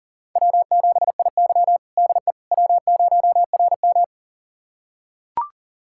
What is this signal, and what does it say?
Morse code, heard as a keyed tone, about 37 wpm, 700 Hz. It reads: W7IY DE W0RM
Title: USA Callsign DE Callsign - 4 Characters - 40wpm